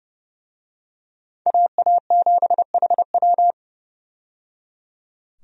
Morse code, heard as a keyed tone, about 30 wpm, 700 Hz. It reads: AA7HW
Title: USA Call Signs - 5 Characters - 30wpm